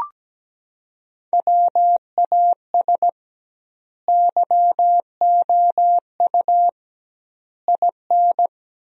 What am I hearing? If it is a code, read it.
Morse code, heard as a keyed tone, about 17 wpm, 700 Hz. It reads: WAS YOU IN
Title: 2x Word-Spacing - Sentences from Top 300 Words - 17wpm